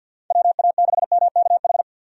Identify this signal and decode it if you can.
Morse code, heard as a keyed tone, about 47 wpm, 700 Hz. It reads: WA6MKH
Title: USA Call Signs - 6 Characters - 50wpm